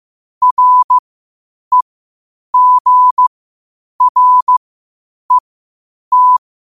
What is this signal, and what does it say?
Morse code, heard as a keyed tone, about 15 wpm, spaced as for 8 wpm, 1000 Hz. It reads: REGRET
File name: qotdfiles / 8wpm